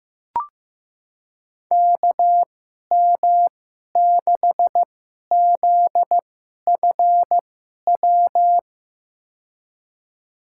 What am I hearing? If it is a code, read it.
Morse code, heard as a keyed tone, about 15 wpm, spaced as for 11 wpm, 700 Hz. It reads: KM6ZFW
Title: USA Call Signs - 6 Characters - 2x Character Spacing - 15wpm